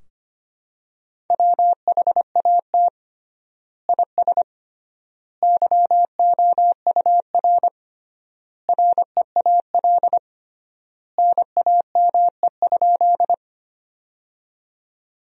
Morse code, heard as a keyed tone, about 25 wpm, 700 Hz. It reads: WHAT IS YOUR REAL NAME?
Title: 3x Word-Spacing - Sentences from Top 300 Words - 25wpm